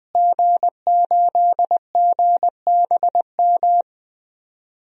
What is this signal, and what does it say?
Morse code, heard as a keyed tone, about 20 wpm, 700 Hz. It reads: G8GBM
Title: DXCC-100 Call Signs - 5 Characters - Repeated 2x - 20wpm